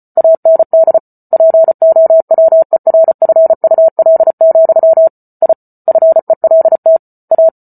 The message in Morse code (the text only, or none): AND POWERFUL, I FELT A